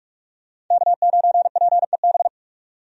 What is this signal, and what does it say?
Morse code, heard as a keyed tone, about 46 wpm, 700 Hz. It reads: K9PEB
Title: USA Call Signs - 5 Characters - 45wpm